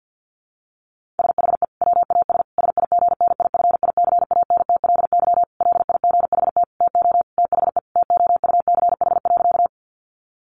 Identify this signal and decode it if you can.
Morse code, heard as a keyed tone, about 50 wpm, 700 Hz. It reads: SHE WAS SIGNIFICANTLY LIGHT TO THE TOUCH.
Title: Sentences from Top 1500 Words 50wpm